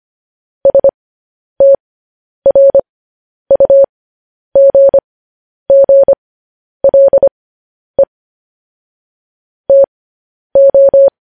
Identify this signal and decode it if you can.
Morse code, heard as a keyed tone, about 25 wpm, spaced as for 10 wpm, 550 Hz. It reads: STRUGGLE TO
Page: news-headlines-in-morse-code-at-10-wpm